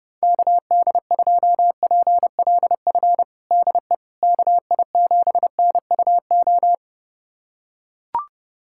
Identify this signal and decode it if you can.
Morse code, heard as a keyed tone, about 30 wpm, 700 Hz. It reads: KD2PLF DE KI7NUO
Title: USA Callsign DE Callsign - 6 Characters - 30wpm